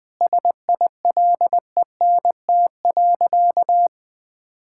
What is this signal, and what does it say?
Morse code, heard as a keyed tone, about 21 wpm, 700 Hz. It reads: SILENT.
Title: Sentences with 7-Word Phrases - 20wpm